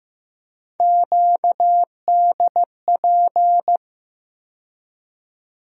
Morse code, heard as a keyed tone, about 15 wpm, 700 Hz. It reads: QDP
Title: USA Call Sign Postfixes - 3 Letters 15wpm